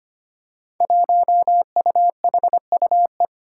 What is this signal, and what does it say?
Morse code, heard as a keyed tone, about 25 wpm, 700 Hz. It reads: 1UHUE
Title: Five Challenging Letter-Number Combinations 25wpm